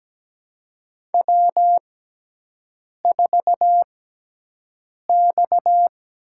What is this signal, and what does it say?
Morse code, heard as a keyed tone, about 17 wpm, spaced as for 6 wpm, 700 Hz. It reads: W4X
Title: USA Call Signs - 6x Character Spacing - 17wpm